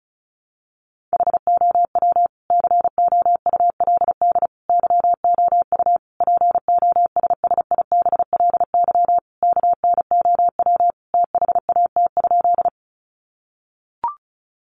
Morse code, heard as a keyed tone, about 34 wpm, 700 Hz. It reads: HOW COULD YOU POSSIBLY KNOW THAT?
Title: Sentences from Top 1300 Words 35wpm